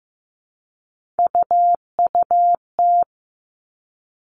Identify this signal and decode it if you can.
Morse code, heard as a keyed tone, about 15 wpm, 700 Hz. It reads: UUT
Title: All Cut Numbers - 3-Digit 15wpm